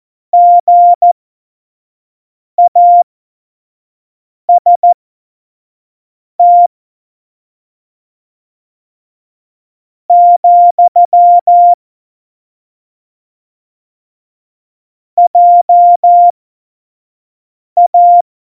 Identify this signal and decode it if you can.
Morse code, heard as a keyed tone, about 14 wpm, spaced as for 4 wpm, 700 Hz. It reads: GAST, JA
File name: Morsekurs / Witze / Witz 22. Oktober 2019, 25 F70